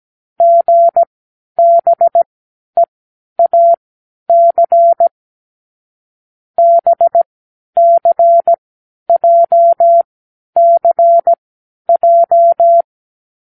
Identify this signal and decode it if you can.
Morse code, heard as a keyed tone, about 17 wpm, spaced as for 10 wpm, 680 Hz. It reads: GBEAC BCJCJ
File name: k7qo / Lesson15(A-J)